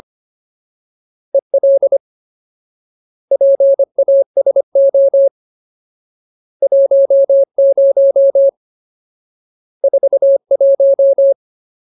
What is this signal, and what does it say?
Morse code, heard as a keyed tone, about 25 wpm, 550 Hz. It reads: EL PASO 10 41
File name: Ultimate Word List / 25WPM Ultimate